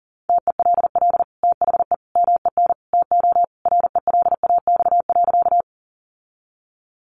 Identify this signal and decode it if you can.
Morse code, heard as a keyed tone, about 40 wpm, 700 Hz. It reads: TELL THE MEN TO RELAX.
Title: Sentences from Top 1400 Words 40wpm